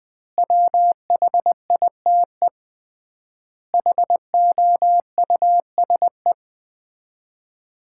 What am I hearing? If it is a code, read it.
Morse code, heard as a keyed tone, about 21 wpm, 700 Hz. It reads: WHITE HOUSE
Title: 3x Word-Spacing - Sets of 4 Words - 20wpm